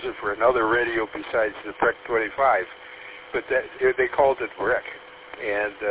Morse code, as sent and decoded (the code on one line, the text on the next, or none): none